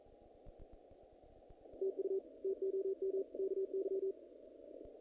Morse code, heard as a keyed tone, about 42 wpm, 390 Hz. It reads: TU TOM KY
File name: contests / naqp-cw-k5tr-op-k5pi- -2000